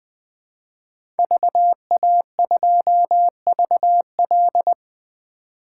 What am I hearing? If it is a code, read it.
Morse code, heard as a keyed tone, about 20 wpm, 700 Hz. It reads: VA2VL
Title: 2023 SOTA Chaser Call Signs - 20wpm